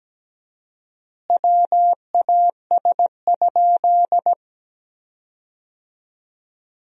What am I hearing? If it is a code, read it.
Morse code, heard as a keyed tone, about 17 wpm, 700 Hz. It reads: WAS?
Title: 4x Word-Spacing - Sentences from Top 200 Words - 17wpm